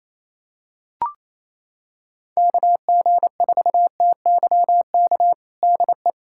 Morse code, heard as a keyed tone, about 28 wpm, 700 Hz. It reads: KG4TYK DE
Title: USA Callsign DE Callsign - 6 Characters - 28wpm